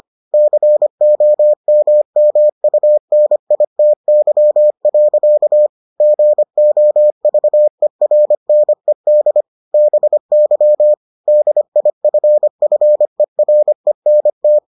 Morse code, heard as a keyed tone, about 25 wpm, 600 Hz. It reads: COMMUNITY. GOVERNED BY DIFFERENT